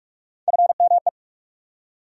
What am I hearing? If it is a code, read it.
Morse code, heard as a keyed tone, about 41 wpm, 700 Hz. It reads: FME
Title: USA Call Sign Postfixes - 3 Letters 45wpm